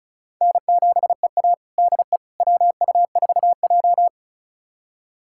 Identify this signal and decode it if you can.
Morse code, heard as a keyed tone, about 36 wpm, 700 Hz. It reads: N7EA DE WU4J
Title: USA Callsign DE Callsign - 4 Characters - 35wpm